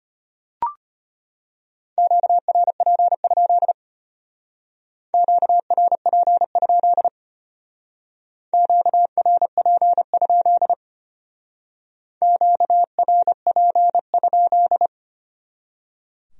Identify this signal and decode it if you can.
Morse code, heard as a keyed tone, about 30 wpm, 700 Hz. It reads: QRP? QRP? QRP? QRP?
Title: Speed-Racing Q-Codes - 25wpm